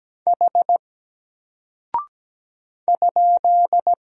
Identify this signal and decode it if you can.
Morse code, heard as a keyed tone, about 17 wpm, 700 Hz. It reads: H ?
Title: Single Character (Letter:Number:Punctuation:Prosign) Alt 17wpm